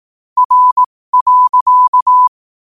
Morse code, heard as a keyed tone, about 18 wpm, spaced as for 15 wpm, 1000 Hz. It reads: R.
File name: qotdfiles / 15wpm